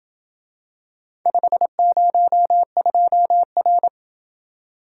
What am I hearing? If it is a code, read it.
Morse code, heard as a keyed tone, about 27 wpm, 700 Hz. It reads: 502R